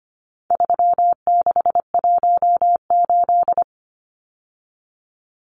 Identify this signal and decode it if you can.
Morse code, heard as a keyed tone, about 25 wpm, 700 Hz. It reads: 3618
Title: Numbers - 4-Digit 25wpm